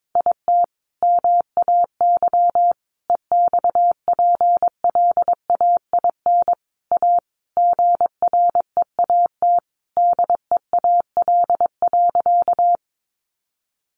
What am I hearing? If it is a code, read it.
Morse code, heard as a keyed tone, about 22 wpm, 700 Hz. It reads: IT MAY EXPLAIN A GREAT DEAL.
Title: Sentences from Top 1100 Words 22wpm